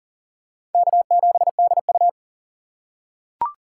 Morse code, heard as a keyed tone, about 40 wpm, 700 Hz. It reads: K7DU
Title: USA Call Signs - 4 Characters - 3x Character Spacing - 40wpm